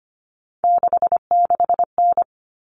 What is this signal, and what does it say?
Morse code, heard as a keyed tone, about 25 wpm, 700 Hz. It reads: 66N
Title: All Cut Numbers - 3-Digit 25wpm